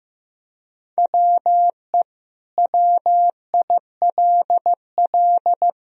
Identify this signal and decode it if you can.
Morse code, heard as a keyed tone, about 16 wpm, 700 Hz. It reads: WE WILL
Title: Sentences from Top 1000 Words - Review Part 4 - 15wpm